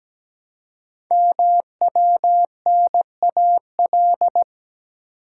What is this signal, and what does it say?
Morse code, heard as a keyed tone, about 17 wpm, 700 Hz. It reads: MWNAL
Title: Code Groups - 5 Letters - 17wpm